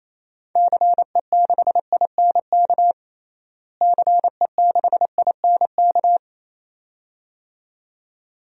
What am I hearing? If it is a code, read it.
Morse code, heard as a keyed tone, about 28 wpm, 700 Hz. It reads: CE6INK CE6INK
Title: DXCC-100 Call Signs - 6 Characters - Repeated 2x - 28wpm